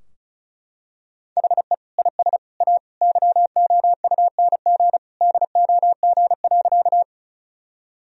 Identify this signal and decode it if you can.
Morse code, heard as a keyed tone, about 35 wpm, 700 Hz. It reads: HE IS A YOUNG DOG.
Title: Sentences from Top 1000 Words - Review Part 2 - 35wpm